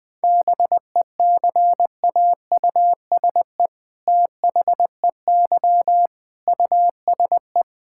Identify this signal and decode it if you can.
Morse code, heard as a keyed tone, about 21 wpm, 700 Hz. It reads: BECAUSE THEY USE
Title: All-Age Appropriate Jokes - 20wpm